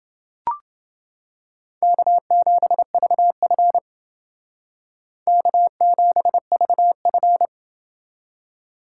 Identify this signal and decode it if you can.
Morse code, heard as a keyed tone, about 29 wpm, 700 Hz. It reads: K7VF K7VF